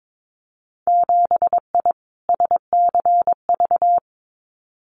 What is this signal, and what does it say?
Morse code, heard as a keyed tone, about 22 wpm, 700 Hz. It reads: 7I SCV
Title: Winter Field Day Exchanges Encore 22wpm